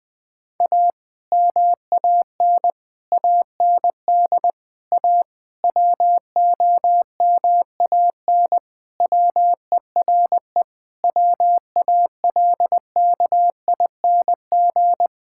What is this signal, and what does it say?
Morse code, heard as a keyed tone, about 19 wpm, 700 Hz. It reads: A MAN AND A WOMAN WERE WALKING
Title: Sentences with 5-Word Phrases - 20wpm